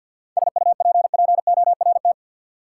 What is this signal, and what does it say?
Morse code, heard as a keyed tone, about 53 wpm, 700 Hz. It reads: SUPPORT
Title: Sentences from Top 800 Words - New Word Drill - 50wpm